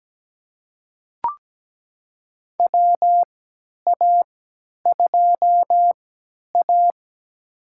Morse code, heard as a keyed tone, about 17 wpm, spaced as for 10 wpm, 700 Hz. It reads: WA2A
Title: USA Call Signs - 3x Character Spacing - 17wpm